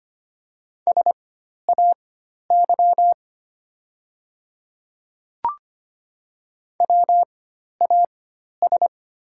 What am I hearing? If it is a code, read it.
Morse code, heard as a keyed tone, about 25 wpm, spaced as for 12 wpm, 700 Hz. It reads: SAY WAS
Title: 4x Character Spacing - Top 200 Words - 25wpm